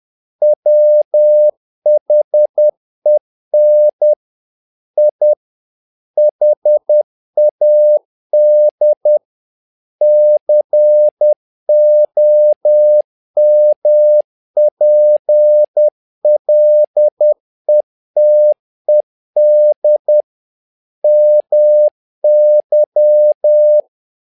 Morse code, heard as a keyed tone, about 10 wpm, 600 Hz. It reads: WHEN I HAD COMPLETED MY
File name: learning center / tower trouble / tower trouble 10wpm